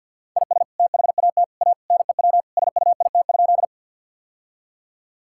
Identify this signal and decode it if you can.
Morse code, heard as a keyed tone, about 50 wpm, 700 Hz. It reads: IS THAT A NEW SUIT?
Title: Sentences from Top 1600 Words 50wpm